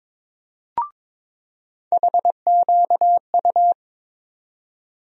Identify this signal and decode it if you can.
Morse code, heard as a keyed tone, about 21 wpm, 700 Hz. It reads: HQU
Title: USA Call Sign Postfixes - 3 Letters 22wpm